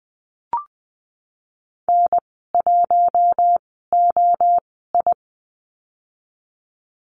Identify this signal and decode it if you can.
Morse code, heard as a keyed tone, about 20 wpm, spaced as for 14 wpm, 700 Hz. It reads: N1OI
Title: 048 Lesson 9g - Number 4 - Call Signs 20wpm